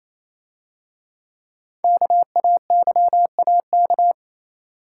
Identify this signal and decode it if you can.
Morse code, heard as a keyed tone, about 27 wpm, 700 Hz. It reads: KAYAK
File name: Easy Words - Speed-Racing with Hints - 22wpm